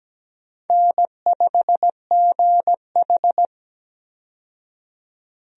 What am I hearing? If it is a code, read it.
Morse code, heard as a keyed tone, about 17 wpm, 700 Hz. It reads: N5GH
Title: USA Call Signs - 4 Characters - 17wpm